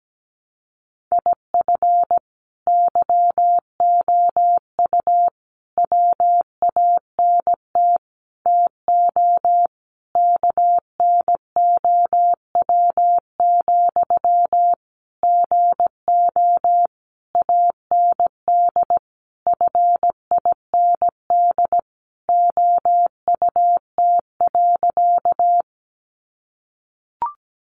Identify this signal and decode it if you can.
Morse code, heard as a keyed tone, about 17 wpm, 700 Hz. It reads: IF YOU WANT TO KNOW, GO AND FIND OUT.
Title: Sentences from Top 200 Words 17wpm